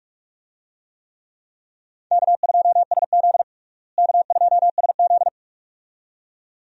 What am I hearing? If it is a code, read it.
Morse code, heard as a keyed tone, about 44 wpm, 700 Hz. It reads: K2SZ K2SZ
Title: USA Call Signs - 4 Characters - Repeated 2x - 45wpm